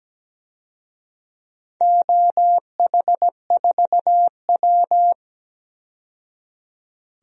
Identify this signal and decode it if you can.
Morse code, heard as a keyed tone, about 17 wpm, 700 Hz. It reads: OH4W
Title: DXCC-100 Call Signs - 4 Characters - 17wpm